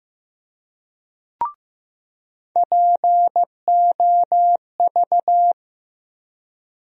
Morse code, heard as a keyed tone, about 15 wpm, 700 Hz. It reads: POV
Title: USA Call Sign Postfixes - 3 Letters 15wpm